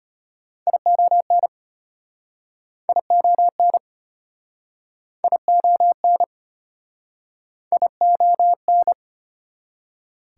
Morse code, heard as a keyed tone, about 37 wpm, 700 Hz. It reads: ION ION ION ION